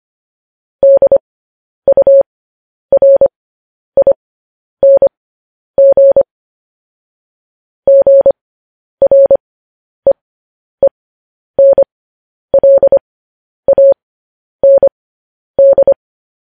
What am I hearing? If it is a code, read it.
Morse code, heard as a keyed tone, about 25 wpm, spaced as for 10 wpm, 550 Hz. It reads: DURING GREENLAND
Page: news-headlines-in-morse-code-at-10-wpm